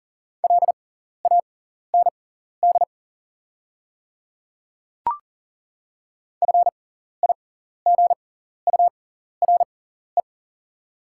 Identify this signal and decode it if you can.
Morse code, heard as a keyed tone, about 40 wpm, spaced as for 14 wpm, 690 Hz. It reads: LAND FIGURE
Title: 6x Character Spacing - Top 500 Words - 40wpm